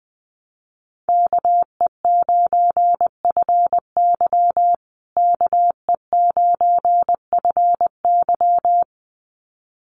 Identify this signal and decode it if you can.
Morse code, heard as a keyed tone, about 20 wpm, 700 Hz. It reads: KE9FY KE9FY
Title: USA Call Signs Repeated 2x 20wpm